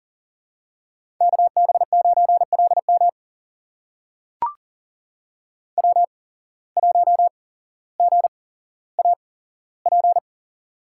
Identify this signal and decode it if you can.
Morse code, heard as a keyed tone, about 40 wpm, spaced as for 17 wpm, 700 Hz. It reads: KB9LM W1GAP
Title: USA Call Signs - 5 Characters - 8x Character Spacing - 40wpm